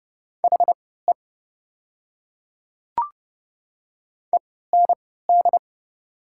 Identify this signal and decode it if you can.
Morse code, heard as a keyed tone, about 30 wpm, spaced as for 17 wpm, 690 Hz. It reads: HE END
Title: 3x Character Spacing - Top 500 Words - 30wpm